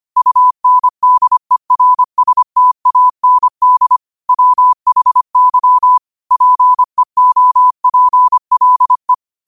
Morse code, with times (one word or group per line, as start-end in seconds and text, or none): 0.2-4.0 UNDERSTAND
4.3-6.0 WHY
6.3-9.1 PEOPLE